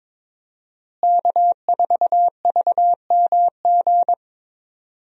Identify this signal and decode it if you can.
Morse code, heard as a keyed tone, about 22 wpm, 700 Hz. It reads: K4VMG
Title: USA Call Signs - 5 Characters - Repeated 2x - 22wpm